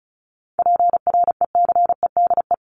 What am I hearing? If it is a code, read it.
Morse code, heard as a keyed tone, about 34 wpm, 700 Hz. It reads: PRECEDE
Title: ING Suffix - Top 100 Words with Base Words 35wpm